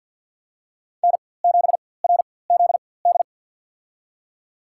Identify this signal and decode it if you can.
Morse code, heard as a keyed tone, about 48 wpm, spaced as for 23 wpm, 700 Hz. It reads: N7RZD
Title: USA Call Signs - 4x Character Spacing - 50wpm